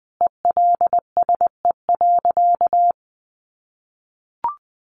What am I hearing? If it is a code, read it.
Morse code, heard as a keyed tone, about 20 wpm, 700 Hz. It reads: ELSE.
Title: Sentences from Top 800 Words 20wpm